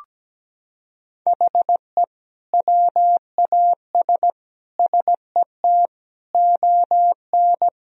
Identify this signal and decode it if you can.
Morse code, heard as a keyed tone, about 17 wpm, 700 Hz. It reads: HE WAS SET ON
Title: Sentences from Top 1000 Words - Review Part 1 - 17wpm